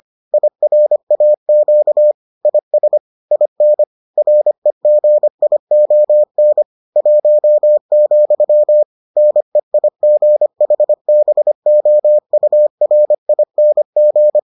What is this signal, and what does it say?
Morse code, heard as a keyed tone, about 25 wpm, 600 Hz. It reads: IRAQ IS IN REGION 1, NEIGHBOURING